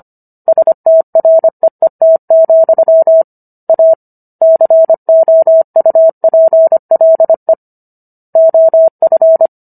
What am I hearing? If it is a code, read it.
Morse code, heard as a keyed tone, about 25 wpm, 650 Hz. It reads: STREET, A COUPLE OF